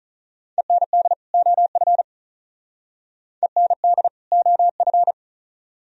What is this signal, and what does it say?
Morse code, heard as a keyed tone, about 47 wpm, 700 Hz. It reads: END OF END OF